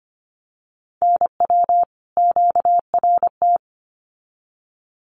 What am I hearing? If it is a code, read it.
Morse code, heard as a keyed tone, about 25 wpm, 700 Hz. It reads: NW QRT
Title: Common QSO Phrases 25wpm